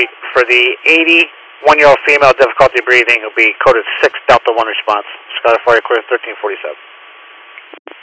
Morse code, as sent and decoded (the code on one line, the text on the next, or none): none